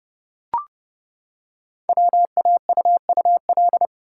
Morse code, heard as a keyed tone, about 30 wpm, 690 Hz. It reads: WAUUL
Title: Code Groups - 5 Letters - 30wpm